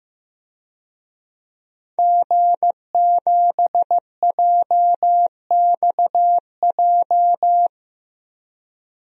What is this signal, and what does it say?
Morse code, heard as a keyed tone, about 15 wpm, 700 Hz. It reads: G7JXJ
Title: DXCC-100 Call Signs - 5 Characters - 15wpm